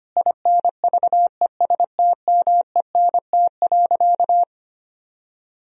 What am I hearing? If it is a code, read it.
Morse code, heard as a keyed tone, about 25 wpm, 700 Hz. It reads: INVESTMENT.
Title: Sentences from Top 1000 Words - Review Part 4 - 25wpm